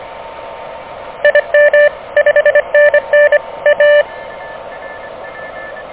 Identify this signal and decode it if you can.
Morse code, heard as a keyed tone, about 25 wpm, 590 Hz. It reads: IM 5NN A
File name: OK QRP 2023 / OK QRP 2023